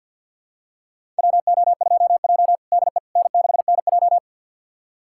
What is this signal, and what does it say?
Morse code, heard as a keyed tone, about 48 wpm, 700 Hz. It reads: WO2J DE N6NJ
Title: USA Callsign DE Callsign - 4 Characters - 50wpm